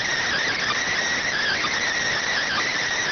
Emergency Alert System (SAME) tones, heard as harsh buzzing bursts. It is heard from 0.2 to 2.8 s.